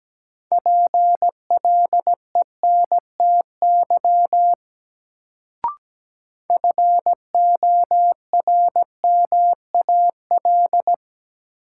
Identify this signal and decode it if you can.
Morse code, heard as a keyed tone, about 17 wpm, 700 Hz. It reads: PLENTY FORMAL
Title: Sentences from Top 1300 Words - New Word Drill - 17wpm